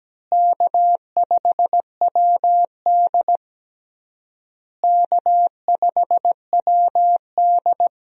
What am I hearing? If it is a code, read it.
Morse code, heard as a keyed tone, about 17 wpm, 700 Hz. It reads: K5WD K5WD
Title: USA Call Signs - 4 Characters - Repeated 3x - 17wpm